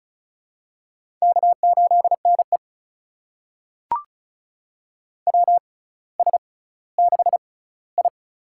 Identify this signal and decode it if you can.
Morse code, heard as a keyed tone, about 34 wpm, spaced as for 18 wpm, 700 Hz. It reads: K8NE WS6I
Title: USA Call Signs - 4 Characters - 6x Character Spacing - 35wpm